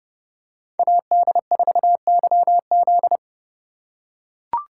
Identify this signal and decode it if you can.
Morse code, heard as a keyed tone, about 30 wpm, 700 Hz. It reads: AD4YZ
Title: USA Call Signs - 5 Characters - 2x Character Spacing - 30wpm